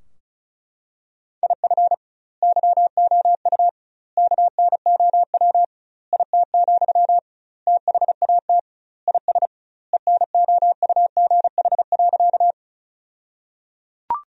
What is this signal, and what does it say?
Morse code, heard as a keyed tone, about 35 wpm, 700 Hz. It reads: IF YOU KNOW IT, THAT IS ENOUGH.
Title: 2x Word-Spacing - Sentences from Top 400 Words - 35wpm